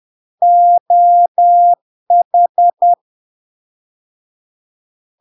Morse code, heard as a keyed tone, about 10 wpm, 700 Hz. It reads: OH
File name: LARK-FD-2022-Log-in-CW-10wpm-700-Hz